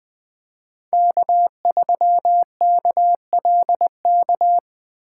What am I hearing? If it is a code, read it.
Morse code, heard as a keyed tone, about 20 wpm, 700 Hz. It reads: K3KLK
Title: Speed Racing - USA 1x3 Call Signs 17wpm